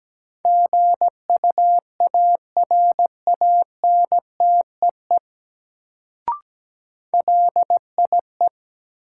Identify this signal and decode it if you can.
Morse code, heard as a keyed tone, about 17 wpm, 700 Hz. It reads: GUARANTEE LIE
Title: Sentences from Top 1600 Words - New Word Drill Alt - 17wpm